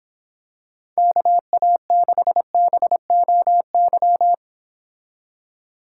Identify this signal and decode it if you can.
Morse code, heard as a keyed tone, about 26 wpm, 700 Hz. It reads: KA6BOY